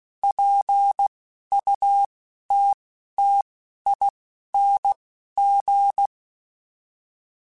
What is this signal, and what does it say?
Morse code, heard as a keyed tone, about 16 wpm, spaced as for 12 wpm, 800 Hz. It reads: PUTTING
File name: Chuckles / Chuckles-mp3s / BrainMouth @10